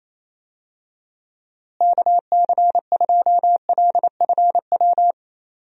Morse code, heard as a keyed tone, about 28 wpm, 700 Hz. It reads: KC2LFW
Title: USA Call Signs - 6 Characters - Repeated 2x - 28wpm